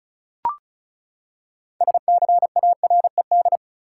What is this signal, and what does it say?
Morse code, heard as a keyed tone, about 34 wpm, 700 Hz. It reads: SCARED
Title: Six Letter Words - 35wpm